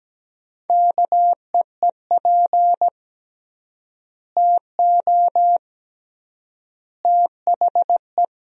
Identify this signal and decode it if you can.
Morse code, heard as a keyed tone, about 17 wpm, 700 Hz. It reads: KEEP TO THE
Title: 3x Word-Spacing - Sentences from Top 300 Words - 17wpm